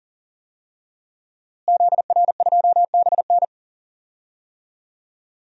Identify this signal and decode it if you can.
Morse code, heard as a keyed tone, about 40 wpm, 700 Hz. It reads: ZR2BN
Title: DXCC-100 Call Signs - 5 Characters - Repeated 3x - 40wpm